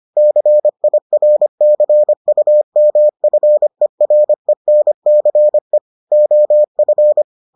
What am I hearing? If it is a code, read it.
Morse code, heard as a keyed tone, about 25 wpm, 600 Hz. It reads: CIRCUMFERENCE OF